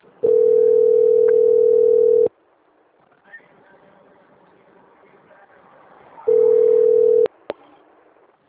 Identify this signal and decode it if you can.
Ringback tone (North America), 440 + 480 Hz.
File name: out- - - - - .20346